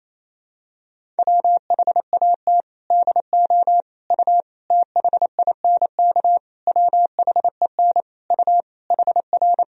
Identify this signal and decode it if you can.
Morse code, heard as a keyed tone, about 28 wpm, 700 Hz. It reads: WHAT DO U THINK WHEN U HR
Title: Sentences from Top 300 Words and Abbreviations 28wpm